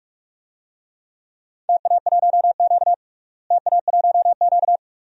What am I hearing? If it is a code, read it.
Morse code, heard as a keyed tone, about 46 wpm, 700 Hz. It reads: TA1Q TA1Q
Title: DXCC-100 Call Signs - 4 Characters - Repeated 2x - 45wpm